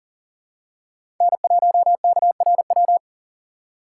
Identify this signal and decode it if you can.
Morse code, heard as a keyed tone, about 40 wpm, 700 Hz. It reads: N1KRW
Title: USA Call Signs - 3x Character Spacing - 40wpm